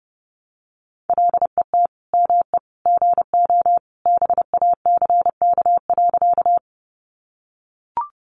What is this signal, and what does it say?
Morse code, heard as a keyed tone, about 30 wpm, 700 Hz. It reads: LET ME GO BACK.
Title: Sentences from Top 300 Words 30wpm